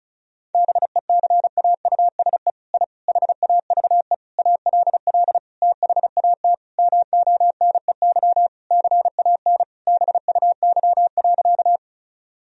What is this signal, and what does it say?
Morse code, heard as a keyed tone, about 34 wpm, 700 Hz. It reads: BECAUSE I HAVE ALL THAT MONEY CAN BUY.
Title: Sentences from Top 1000 Words - Review Part 4 - 35wpm